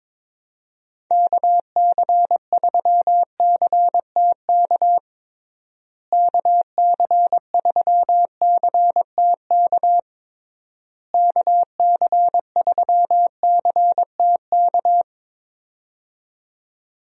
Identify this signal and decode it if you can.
Morse code, heard as a keyed tone, about 22 wpm, 700 Hz. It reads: KC3CTK KC3CTK KC3CTK
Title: USA Call Signs - 6 Characters - Repeated 3x - 22wpm